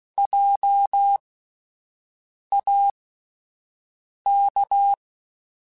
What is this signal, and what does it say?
Morse code, heard as a keyed tone, about 16 wpm, spaced as for 5 wpm, 800 Hz. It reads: JAK